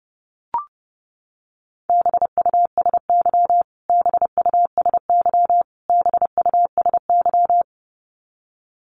Morse code, heard as a keyed tone, about 30 wpm, 700 Hz. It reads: BUSY BUSY BUSY
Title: Top 500 Words Repeated 30wpm